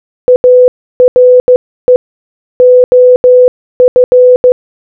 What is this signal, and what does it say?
Morse code, heard as a keyed tone, about 15 wpm, 500 Hz. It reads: ARE OF